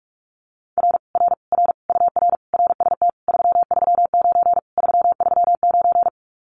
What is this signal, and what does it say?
Morse code, heard as a keyed tone, about 46 wpm, 700 Hz. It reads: R R R UR RST 339 339
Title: Common QSO Phrases Encore 45wpm